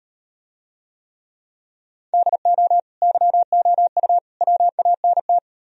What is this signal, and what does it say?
Morse code, heard as a keyed tone, about 38 wpm, 700 Hz. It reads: DO YOU WANT